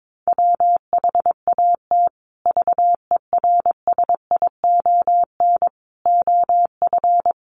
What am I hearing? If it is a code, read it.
Morse code, heard as a keyed tone, about 22 wpm, 700 Hz. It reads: WHAT VERSION OF
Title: Sentences from Top 1400 Words 22wpm